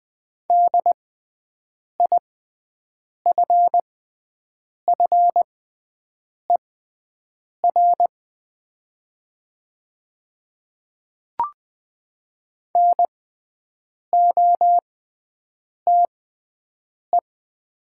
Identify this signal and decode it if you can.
Morse code, heard as a keyed tone, about 20 wpm, spaced as for 7 wpm, 700 Hz. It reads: DIFFER NOTE
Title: 6x Character Spacing - Top 500 Words - 20wpm